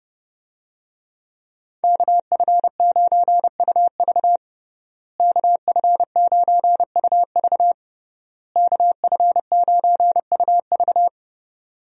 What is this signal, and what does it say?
Morse code, heard as a keyed tone, about 30 wpm, 700 Hz. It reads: KF9UV KF9UV KF9UV
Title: USA Call Signs - 5 Characters - Repeated 3x - 30wpm